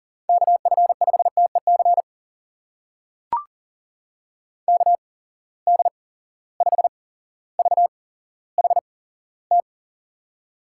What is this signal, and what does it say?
Morse code, heard as a keyed tone, about 39 wpm, spaced as for 17 wpm, 700 Hz. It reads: KF5TEC KD5VHT
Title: USA Call Signs - 6 Characters - 8x Character Spacing - 40wpm